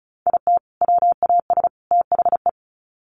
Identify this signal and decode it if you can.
Morse code, heard as a keyed tone, about 34 wpm, 700 Hz. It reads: IT WAS THE
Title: Sets of 3 Words 35wpm